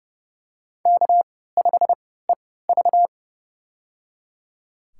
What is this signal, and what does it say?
Morse code, heard as a keyed tone, about 30 wpm, spaced as for 17 wpm, 700 Hz. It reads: K5EV
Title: USA Call Signs - 4 Characters - 3x Character Spacing - 30wpm